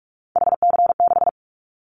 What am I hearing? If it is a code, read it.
Morse code, heard as a keyed tone, about 46 wpm, 700 Hz. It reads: HC6
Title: Callsign Prefixes for Top 100 DXCC - 45wpm